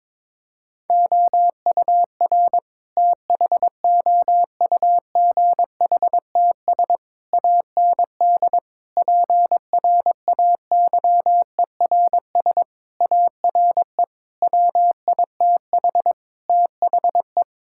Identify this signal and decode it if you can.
Morse code, heard as a keyed tone, about 22 wpm, 700 Hz. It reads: OUR THOUGHTS AND PRAYERS ARE WITH THE
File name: Sentences with 7-Word Phrases - 22wpm